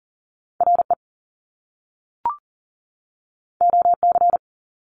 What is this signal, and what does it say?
Morse code, heard as a keyed tone, about 39 wpm, 700 Hz. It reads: RE OC
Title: Top Two Letter Combinations in Words 40wpm